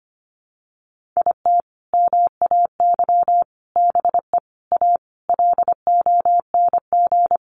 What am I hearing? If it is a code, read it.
Morse code, heard as a keyed tone, about 25 wpm, 700 Hz. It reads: IT MAY BE A LONG
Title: Sentences from Top 1100 Words 25wpm